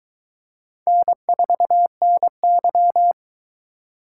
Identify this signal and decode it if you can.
Morse code, heard as a keyed tone, about 22 wpm, 700 Hz. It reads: N4NY